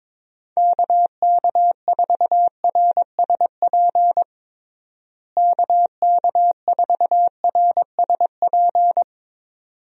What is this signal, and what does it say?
Morse code, heard as a keyed tone, about 22 wpm, 700 Hz. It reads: KK4RSP KK4RSP
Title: USA Call Signs - 6 Characters - Repeated 2x - 22wpm